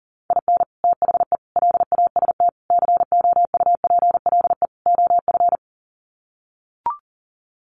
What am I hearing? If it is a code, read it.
Morse code, heard as a keyed tone, about 40 wpm, 700 Hz. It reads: IN THE LAST COUPLE OF
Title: Sets of 5 Words 40wpm